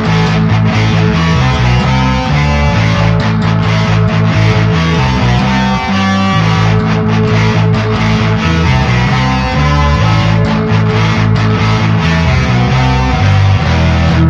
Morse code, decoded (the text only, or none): none